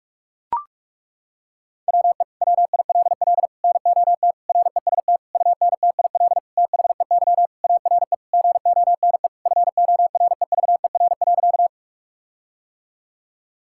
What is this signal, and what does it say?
Morse code, heard as a keyed tone, about 41 wpm, 700 Hz. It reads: WE WILL NOT REST UNTIL THEY ARE GONE FOREVER.
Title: Sentences from Top 1600 Words 45wpm